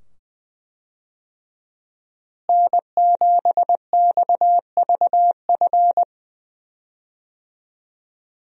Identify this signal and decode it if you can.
Morse code, heard as a keyed tone, about 20 wpm, 700 Hz. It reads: N7XVF
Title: USA Call Signs - 5 Characters - Repeated 2x - 20wpm